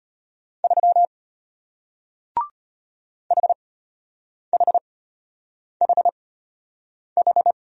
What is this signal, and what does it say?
Morse code, heard as a keyed tone, about 33 wpm, spaced as for 8 wpm, 700 Hz. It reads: 3 HHHH